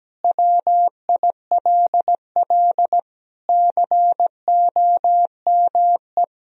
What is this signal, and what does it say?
Morse code, heard as a keyed tone, about 17 wpm, 700 Hz. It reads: WILL COME
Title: Sentences from Top 1000 Words - Review Part 3 - 17wpm